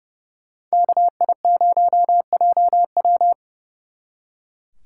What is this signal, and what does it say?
Morse code, heard as a keyed tone, about 36 wpm, 700 Hz. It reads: KI0JW
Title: Speed Racing - USA Call Signs 30wpm